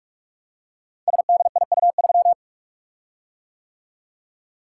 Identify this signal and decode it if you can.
Morse code, heard as a keyed tone, about 44 wpm, 700 Hz. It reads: SDIU3
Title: Code Groups - 5 Letter:Numbers - 45wpm